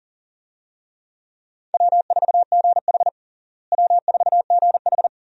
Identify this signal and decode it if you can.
Morse code, heard as a keyed tone, about 40 wpm, 700 Hz. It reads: W4GH W4GH
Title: USA Call Signs - 4 Characters - Repeated 3x - 40wpm